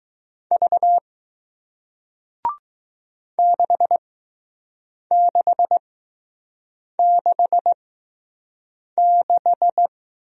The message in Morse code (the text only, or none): V 6666